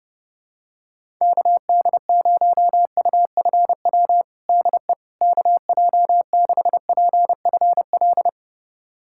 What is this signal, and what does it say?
Morse code, heard as a keyed tone, about 32 wpm, 700 Hz. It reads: KD0UFW DE KJ6PFL
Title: USA Callsign DE Callsign - 6 Characters - 30wpm